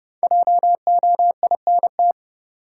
Morse code, heard as a keyed tone, about 31 wpm, 700 Hz. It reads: JOINT
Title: Sentences from Top 1300 Words - New Word Drill Alt - 30wpm